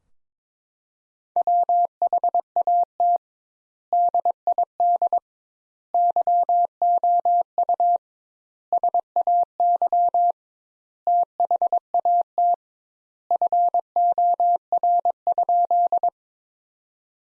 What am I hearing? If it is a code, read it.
Morse code, heard as a keyed tone, about 21 wpm, 700 Hz. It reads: WHAT DID YOU SAY THAT FOR?
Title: Sentences Top 200 Words - 22wpm